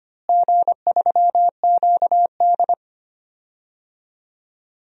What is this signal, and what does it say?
Morse code, heard as a keyed tone, about 25 wpm, 700 Hz. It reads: G3QD
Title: DXCC-100 Call Signs - 4 Characters - Repeated 3x - 25wpm